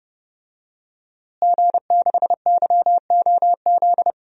Morse code, heard as a keyed tone, about 30 wpm, 700 Hz. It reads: G6YOZ
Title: DXCC-100 Call Signs - 5 Characters - Repeated 2x - 30wpm